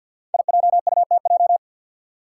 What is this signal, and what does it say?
Morse code, heard as a keyed tone, about 52 wpm, 700 Hz. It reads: IJUTJ
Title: Five Challenging Letter-Number Combinations 50wpm